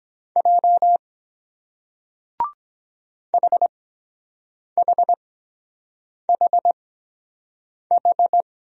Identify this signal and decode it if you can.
Morse code, heard as a keyed tone, about 23 wpm, spaced as for 7 wpm, 700 Hz. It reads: J HHHH